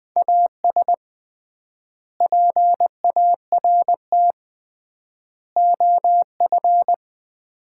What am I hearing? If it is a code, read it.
Morse code, heard as a keyed tone, about 21 wpm, 700 Hz. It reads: AS PART OF
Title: 3x Word-Spacing - Sets of 3 Words - 20wpm